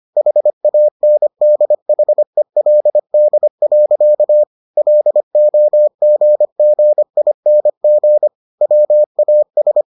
Morse code, heard as a keyed tone, about 25 wpm, 600 Hz. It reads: HANDHELD. LOGGING WAS